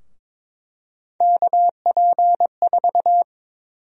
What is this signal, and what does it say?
Morse code, heard as a keyed tone, about 22 wpm, 700 Hz. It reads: KP4
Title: USA Call Sign Prefixes - 22wpm